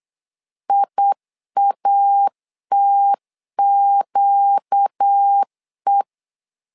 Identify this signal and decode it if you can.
Morse code, heard as a keyed tone, about 8 wpm, 800 Hz. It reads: IATQE
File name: 4 Q 8wpm